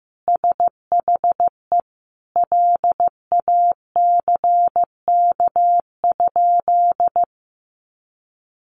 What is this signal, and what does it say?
Morse code, heard as a keyed tone, about 14 wpm, 700 Hz. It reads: SHE LACK?
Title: Sentences from Top 1100 Words 15wpm